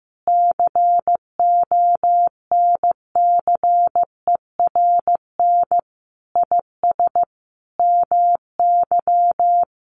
Morse code, heard as a keyed tone, about 15 wpm, 700 Hz. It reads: CONCERN IS MY
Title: Sentences from Top 1200 Words 15wpm